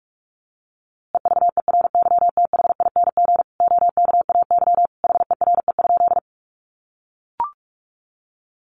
Morse code, heard as a keyed tone, about 44 wpm, 700 Hz. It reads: EVERYTHING OKAY HERE?
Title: A Sound Of Thunder 45wpm